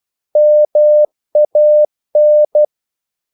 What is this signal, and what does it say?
Morse code, heard as a keyed tone, about 12 wpm, 600 Hz. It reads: MAN
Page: English quotations in CW (telegraphy - morse)